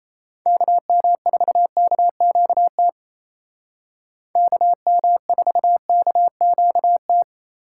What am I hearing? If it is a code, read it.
Morse code, heard as a keyed tone, about 33 wpm, 700 Hz. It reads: KM4KQT KM4KQT
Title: Speed Racing - USA 2x3 Call Signs 28wpm